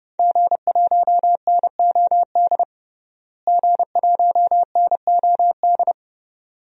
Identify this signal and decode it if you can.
Morse code, heard as a keyed tone, about 30 wpm, 700 Hz. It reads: G1NOD G1NOD
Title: DXCC-100 Call Signs - 5 Characters - Repeated 2x - 30wpm